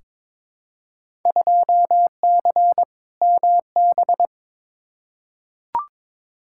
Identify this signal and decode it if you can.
Morse code, heard as a keyed tone, about 22 wpm, 700 Hz. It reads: 2C MB
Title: Field Day Exchanges Encore - 22wpm